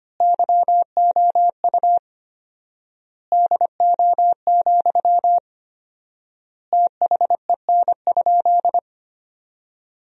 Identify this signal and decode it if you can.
Morse code, heard as a keyed tone, about 25 wpm, 700 Hz. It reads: YOU DO, THEN?
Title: 4x Word-Spacing - Sentences from Top 100 Words - 25wpm